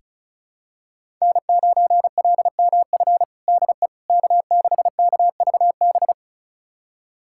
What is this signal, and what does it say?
Morse code, heard as a keyed tone, about 36 wpm, 700 Hz. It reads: N9LMF DE K6KVB
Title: USA Callsign DE Callsign - 5 Characters - 35wpm